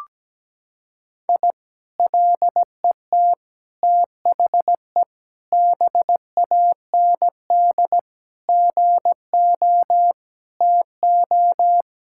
Morse code, heard as a keyed tone, about 17 wpm, 700 Hz. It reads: I LET THE BAND GO TO
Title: Sentences from Top 1600 Words 17wpm